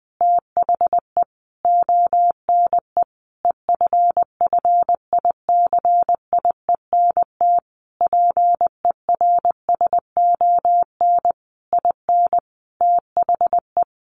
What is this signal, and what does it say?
Morse code, heard as a keyed tone, about 19 wpm, 700 Hz. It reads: THE ONE EFFICIENT PERSON IN THE
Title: Sentences from Top 1300 Words 20wpm